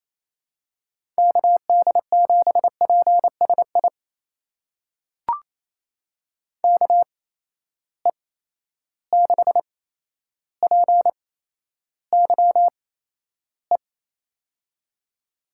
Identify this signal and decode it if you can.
Morse code, heard as a keyed tone, about 27 wpm, spaced as for 12 wpm, 700 Hz. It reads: KD7PSI KE6PYE
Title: USA Call Signs - 6 Characters - 8x Character Spacing - 28wpm